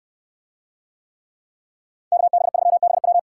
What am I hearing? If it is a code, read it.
Morse code, heard as a keyed tone, about 68 wpm, 700 Hz. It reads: KB3BW